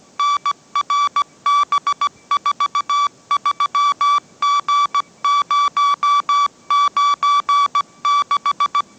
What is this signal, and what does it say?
Morse code, heard as a keyed tone, about 17 wpm, 1150 Hz. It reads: NRB43G096
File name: mrx / mrx080 18wpm